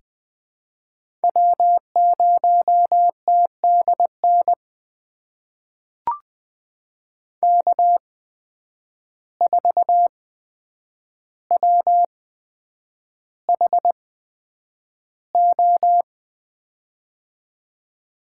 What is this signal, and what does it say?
Morse code, heard as a keyed tone, about 20 wpm, spaced as for 9 wpm, 700 Hz. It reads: W0TDN K4WHO
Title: USA Call Signs - 5 Characters - 8x Character Spacing - 20wpm